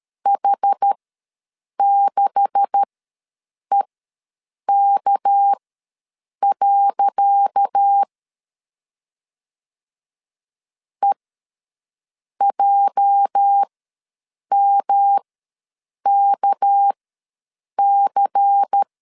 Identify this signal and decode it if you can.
Morse code, heard as a keyed tone, about 13 wpm, spaced as for 7 wpm, 800 Hz. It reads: H6EK. EJMKC
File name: LEZIONI / 10 SIGNS / 10C DOT ws 12wpm 3548